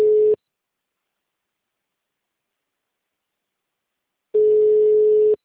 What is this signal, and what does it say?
Ringback tone (Europe, 425 Hz).
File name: out- -2033- - - .903